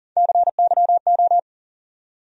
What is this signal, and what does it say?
Morse code, heard as a keyed tone, about 40 wpm, 700 Hz. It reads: CYO
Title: USA Call Sign Postfixes - 3 Letters 40wpm